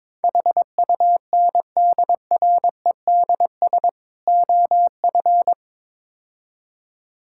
Morse code, heard as a keyed tone, about 22 wpm, 700 Hz. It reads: HUNDREDS OF
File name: practice / 22wpm / Build A Phrase - 5-Word Phrases - 22wpm